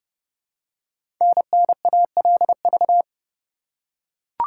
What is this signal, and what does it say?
Morse code, heard as a keyed tone, about 30 wpm, 700 Hz. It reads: NNALV
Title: Code Groups - 5 Letter:Numbers - 30wpm